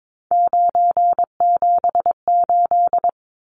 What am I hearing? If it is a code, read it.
Morse code, heard as a keyed tone, about 22 wpm, 700 Hz. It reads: 978
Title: Numbers - 3-Digit 22wpm